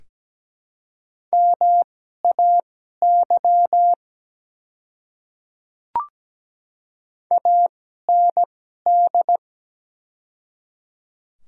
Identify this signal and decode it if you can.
Morse code, heard as a keyed tone, about 17 wpm, spaced as for 12 wpm, 690 Hz. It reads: MAY AND
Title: 2x Character Spacing - Top 100 Words - 17wpm